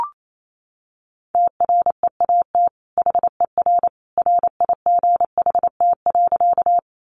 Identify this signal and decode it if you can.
Morse code, heard as a keyed tone, about 28 wpm, 700 Hz. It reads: TREAT HER RIGHT.
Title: Sentences from Top 1300 Words 28wpm